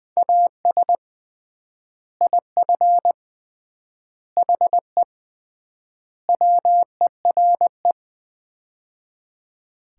Morse code, heard as a keyed tone, about 20 wpm, 700 Hz. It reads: AS IF HE WERE
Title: 3x Word-Spacing - Sets of 4 Words - 20wpm